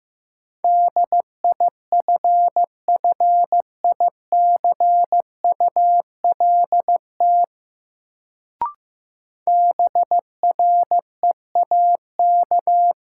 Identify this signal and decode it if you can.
Morse code, heard as a keyed tone, about 15 wpm, 700 Hz. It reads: DIFFICULT BREAK
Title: Sets of 4 Words - New Word Drill - 15wpm